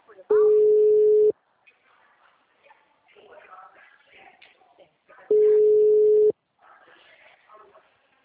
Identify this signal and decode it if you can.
Ringback tone (Europe, 425 Hz), with speech.